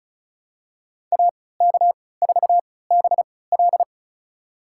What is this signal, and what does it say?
Morse code, heard as a keyed tone, about 36 wpm, spaced as for 20 wpm, 700 Hz. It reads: AK4BL
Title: USA Call Signs - 5 Characters - 3x Character Spacing - 35wpm